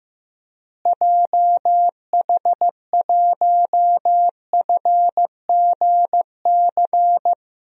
Morse code, heard as a keyed tone, about 15 wpm, 700 Hz. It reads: JH1FGC
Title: DXCC-100 Call Signs - 6 Characters - Repeated 3x - 15wpm